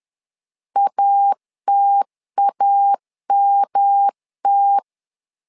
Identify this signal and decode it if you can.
Morse code, heard as a keyed tone, about 11 wpm, 800 Hz. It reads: ATAMT
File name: LEZIONI / 3 MWRO / 3 M 10wpm 1828